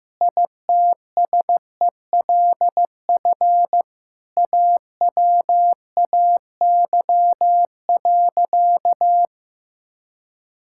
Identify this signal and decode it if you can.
Morse code, heard as a keyed tone, about 15 wpm, 700 Hz. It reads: ITSELF AWAY.
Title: Sentences from Top 1000 Words - Review Part 5 - 15wpm